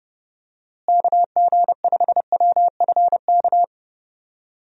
Morse code, heard as a keyed tone, about 32 wpm, 700 Hz. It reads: KG5WFK
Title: USA Call Signs - 6 Characters - Repeated 2x - 30wpm